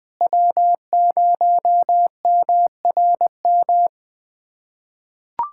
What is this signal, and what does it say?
Morse code, heard as a keyed tone, about 21 wpm, 700 Hz. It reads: W0MRM
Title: USA Callsign DE Callsign - 5 Characters - 20wpm